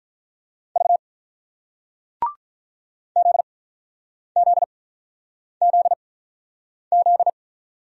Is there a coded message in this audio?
Yes: Morse code from 0.8 to 7.3 s, heard as a keyed tone, about 48 wpm, spaced as for 9 wpm, 700 Hz. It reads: V ZZZZ